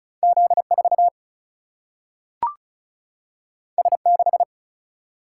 Morse code, heard as a keyed tone, about 36 wpm, 700 Hz. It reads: Z4 S6
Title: Single-Letter Single-Number 35wpm